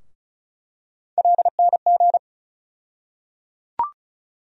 Morse code, heard as a keyed tone, about 36 wpm, 700 Hz. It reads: LNG
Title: USA Call Sign Postfixes - 3 Letters 35wpm